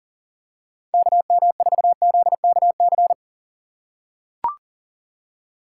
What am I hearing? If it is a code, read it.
Morse code, heard as a keyed tone, about 39 wpm, 700 Hz. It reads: KM4ZKC
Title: USA Call Signs - 6 Characters - 4x Character Spacing - 40wpm